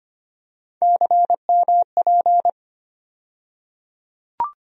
Morse code, heard as a keyed tone, about 25 wpm, 700 Hz. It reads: CMP
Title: USA Call Sign Postfixes - 3 Letters 25wpm